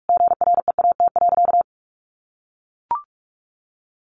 Morse code, heard as a keyed tone, about 44 wpm, 700 Hz. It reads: GREAT.
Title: Sentences from Top 1400 Words 45wpm